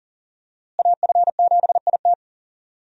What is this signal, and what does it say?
Morse code, heard as a keyed tone, about 40 wpm, 700 Hz. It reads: AF7IT
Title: USA Call Signs - 5 Characters - Repeated 3x - 40wpm